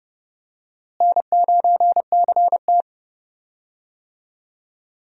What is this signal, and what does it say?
Morse code, heard as a keyed tone, about 30 wpm, 700 Hz. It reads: N9CT
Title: USA Call Signs - 4 Characters - Repeated 3x - 30wpm